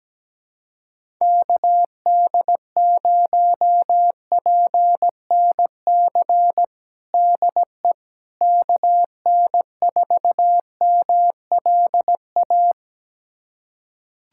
Morse code, heard as a keyed tone, about 17 wpm, 700 Hz. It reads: KD0PNC DE KN4MLA
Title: USA Callsign DE Callsign - 6 Characters - 17wpm